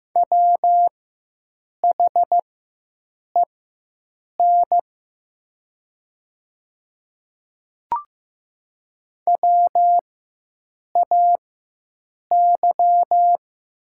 Morse code, heard as a keyed tone, about 15 wpm, spaced as for 7 wpm, 700 Hz. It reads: WHEN WAY
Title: 4x Character Spacing - Top 100 Words - 15wpm